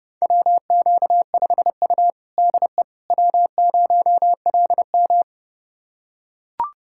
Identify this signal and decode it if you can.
Morse code, heard as a keyed tone, about 32 wpm, 700 Hz. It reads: WQ5U DE W0LM
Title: USA Callsign DE Callsign - 4 Characters - 30wpm